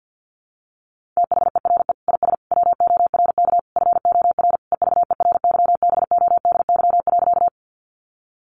Morse code, heard as a keyed tone, about 50 wpm, 700 Hz. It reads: THERE IS WORK FOR EVERYBODY.
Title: Sentences from Top 1400 Words 50wpm